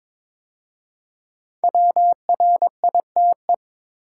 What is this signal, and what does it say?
Morse code, heard as a keyed tone, about 22 wpm, 690 Hz. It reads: WRITE
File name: 5 Letter Words - First 3 Letters Given - 22wpm